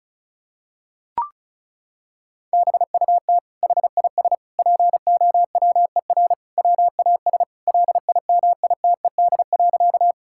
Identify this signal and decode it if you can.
Morse code, heard as a keyed tone, about 34 wpm, 700 Hz. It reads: BUT HIS POWER WAS LIMITED.
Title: Sentences from Top 1000 Words - Review Part 6 - 35wpm